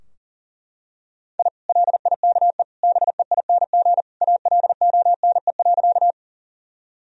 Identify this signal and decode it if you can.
Morse code, heard as a keyed tone, about 40 wpm, 700 Hz. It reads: I LIKE BEING ALONE.
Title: Sentences from Top 1000 Words - Review Part 3 - 40wpm